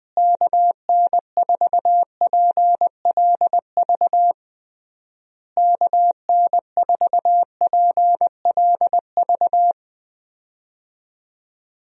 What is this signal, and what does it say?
Morse code, heard as a keyed tone, about 21 wpm, 700 Hz. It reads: KN4PLV KN4PLV
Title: USA Call Signs - 6 Characters - Repeated 3x - 20wpm